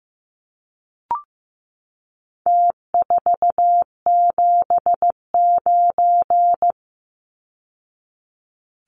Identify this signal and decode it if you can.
Morse code, heard as a keyed tone, about 15 wpm, 700 Hz. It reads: T479
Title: Numbers - Contest Style - 4-Digit 15wpm